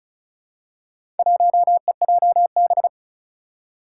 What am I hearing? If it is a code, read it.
Morse code, heard as a keyed tone, about 36 wpm, 700 Hz. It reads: 1EJB
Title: Four Challenging Letter-Number Combinations 35wpm